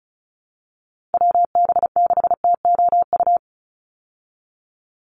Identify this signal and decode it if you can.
Morse code, heard as a keyed tone, about 34 wpm, 700 Hz. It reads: WB6TOU
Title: Call Signs Repeated 2x - 35wpm